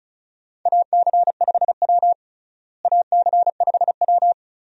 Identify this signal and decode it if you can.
Morse code, heard as a keyed tone, about 34 wpm, 700 Hz. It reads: AC5W AC5W
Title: USA Call Signs - 4 Characters - Repeated 2x - 35wpm